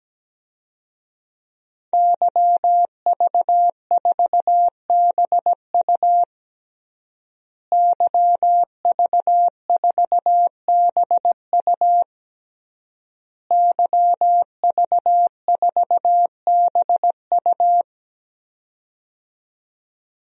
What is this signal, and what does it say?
Morse code, heard as a keyed tone, about 17 wpm, 700 Hz. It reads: YV4BU YV4BU YV4BU
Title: DXCC-100 Call Signs - 5 Characters - Repeated 3x - 17wpm